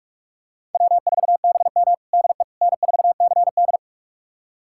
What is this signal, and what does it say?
Morse code, heard as a keyed tone, about 41 wpm, 700 Hz. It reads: W4BM DE N4CD
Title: USA Callsign DE Callsign - 4 Characters - 45wpm